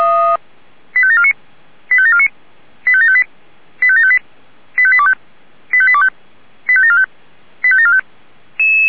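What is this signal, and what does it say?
Five-tone selective call (ZVEI), heard as quick runs of five tones, tones of about 80 ms. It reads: 85749 85749 85758 85758 85825 85825 85745 85745
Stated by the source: Received by SDR; ZVEI five-tone code 85827.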